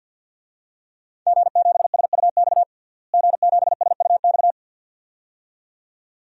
Morse code, heard as a keyed tone, about 50 wpm, 700 Hz. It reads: G7SUX G7SUX
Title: DXCC-100 Call Signs - 5 Characters - Repeated 2x - 50wpm